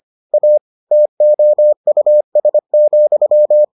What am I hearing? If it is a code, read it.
Morse code, heard as a keyed tone, about 25 wpm, 600 Hz. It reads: A TOUS,
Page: bulletin UFT sous forme texte et en CW